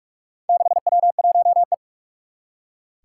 Morse code, heard as a keyed tone, about 44 wpm, 700 Hz. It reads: 6W1E
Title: Four Challenging Letter-Number Combinations 45wpm